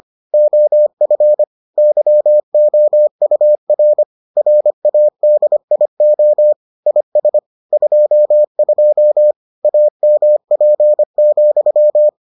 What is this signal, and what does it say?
Morse code, heard as a keyed tone, about 25 wpm, 600 Hz. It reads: OF YOUR RADIO IS 22 AMP,